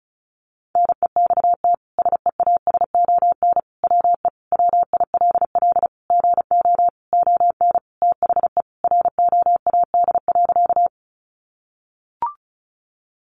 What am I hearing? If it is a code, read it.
Morse code, heard as a keyed tone, about 34 wpm, 700 Hz. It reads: NEXT SEASON WE WILL GO ON THE ROAD.
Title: Sentences from Top 1400 Words 35wpm